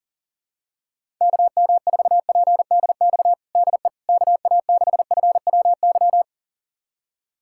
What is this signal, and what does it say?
Morse code, heard as a keyed tone, about 37 wpm, 700 Hz. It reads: KM4PDX DE KA6FWY
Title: USA Callsign DE Callsign - 6 Characters - 40wpm